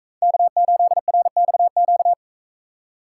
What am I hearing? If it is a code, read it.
Morse code, heard as a keyed tone, about 40 wpm, 700 Hz. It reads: K8RXQ